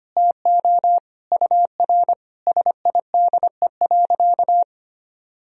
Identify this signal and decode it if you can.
Morse code, heard as a keyed tone, about 25 wpm, 700 Hz. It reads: TO UR SIDE.
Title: Sentences from Top 400 Words and Abbreviations 25wpm